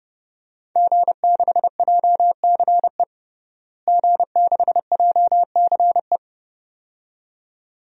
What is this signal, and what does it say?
Morse code, heard as a keyed tone, about 31 wpm, 700 Hz. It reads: G6JCE G6JCE
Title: DXCC-100 Call Signs - 5 Characters - Repeated 2x - 30wpm